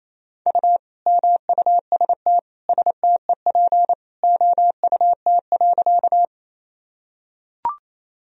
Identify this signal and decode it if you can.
Morse code, heard as a keyed tone, about 28 wpm, 700 Hz. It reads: U MUST STEP OUT.
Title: Sentences from Top 400 Words and Abbreviations 28wpm